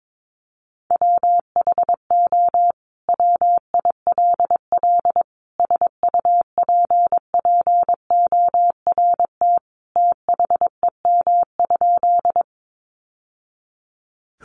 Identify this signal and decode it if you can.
Morse code, heard as a keyed tone, about 22 wpm, 700 Hz. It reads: WHO WILL SUPPORT THEM?
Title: Sentences from Top 800 Words 22wpm